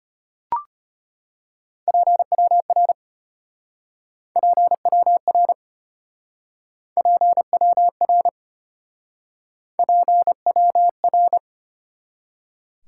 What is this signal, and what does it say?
Morse code, heard as a keyed tone, about 34 wpm, 700 Hz. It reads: PWR PWR PWR PWR